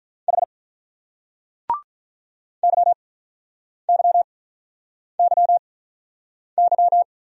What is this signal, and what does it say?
Morse code, heard as a keyed tone, about 43 wpm, spaced as for 8 wpm, 700 Hz. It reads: H YYYY
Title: Speed-Racing Single Letters 35wpm